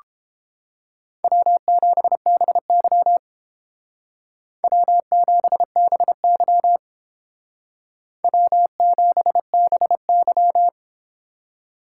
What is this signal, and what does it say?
Morse code, heard as a keyed tone, about 29 wpm, 700 Hz. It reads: W7BY W7BY W7BY